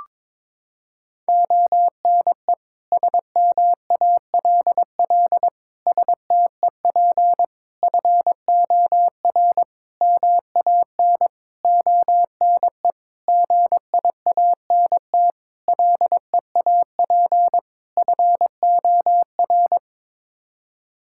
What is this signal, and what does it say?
Morse code, heard as a keyed tone, about 22 wpm, 700 Hz. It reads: ONE SMALL STEP FOR MAN ONE GIANT LEAP FOR
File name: Sets of 9 Words - 22wpm